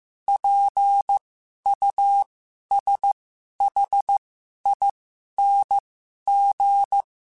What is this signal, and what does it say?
Morse code, heard as a keyed tone, about 15 wpm, spaced as for 11 wpm, 800 Hz. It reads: PUSHING
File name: Special INGs-mp3s / Special INGs-mp3s / Sngl Ing @10